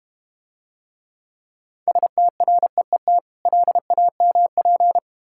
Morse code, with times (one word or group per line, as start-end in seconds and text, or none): 1.9-3.2 STREET
3.5-5.0 LAMP